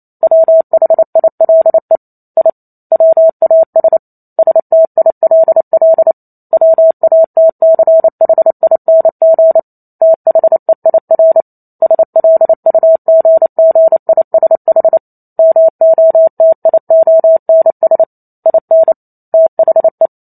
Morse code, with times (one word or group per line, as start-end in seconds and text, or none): 0.2-2.0 WHILE
2.4-2.5 I
2.9-4.0 WAS
4.4-6.1 STILL
6.5-9.6 WATCHING
10.0-11.4 THEIR
11.8-15.0 SLUGGISH
15.4-18.0 MOTIONS
18.5-18.9 IN
19.3-20.1 THE